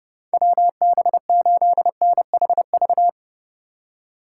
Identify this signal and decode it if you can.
Morse code, heard as a keyed tone, about 32 wpm, 700 Hz. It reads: WB8NHV
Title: USA Call Signs - 6 Characters - 4x Character Spacing - 30wpm